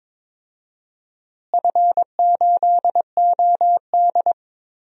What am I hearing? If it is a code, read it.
Morse code, heard as a keyed tone, about 22 wpm, 700 Hz. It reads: F8OD
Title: DXCC-100 Call Signs - 4 Characters - Repeated 2x - 22wpm